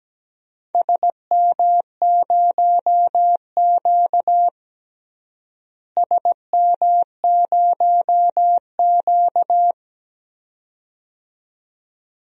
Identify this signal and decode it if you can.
Morse code, heard as a keyed tone, about 17 wpm, 700 Hz. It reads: SM0Q SM0Q
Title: DXCC-100 Call Signs - 4 Characters - Repeated 2x - 17wpm